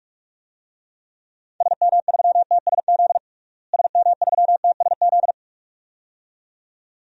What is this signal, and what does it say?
Morse code, heard as a keyed tone, about 46 wpm, 700 Hz. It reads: SM3TSZ SM3TSZ
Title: DXCC-100 Call Signs - 6 Characters - Repeated 2x - 45wpm